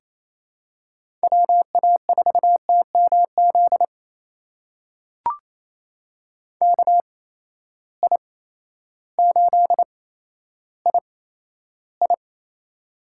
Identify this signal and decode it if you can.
Morse code, heard as a keyed tone, about 27 wpm, spaced as for 13 wpm, 700 Hz. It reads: WA4TMZ KI8II
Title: USA Call Signs - 8x Character Spacing - 28wpm